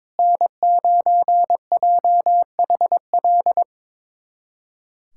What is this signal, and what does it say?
Morse code, heard as a keyed tone, about 22 wpm, 700 Hz. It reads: N9JHL
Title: USA Callsign DE Callsign - 5 Characters - 22wpm